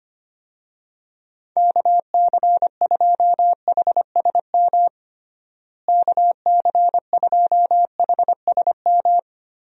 Morse code, heard as a keyed tone, about 25 wpm, 700 Hz. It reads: KC2HSM KC2HSM
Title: USA Call Signs - 6 Characters - Repeated 2x - 25wpm